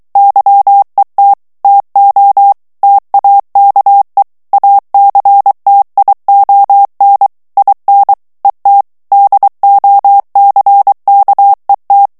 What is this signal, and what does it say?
Morse code, heard as a keyed tone, about 23 wpm, 800 Hz. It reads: YET TO TAKE ACTION IN ET DOCKET
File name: ANGLAIS texte clair 3 vitesse 22 mots minute